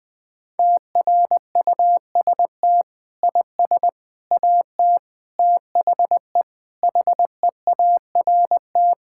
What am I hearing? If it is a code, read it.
Morse code, heard as a keyed tone, about 20 wpm, 700 Hz. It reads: TRUST IS AT THE HEART
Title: Sentences with 5-Word Phrases - 20wpm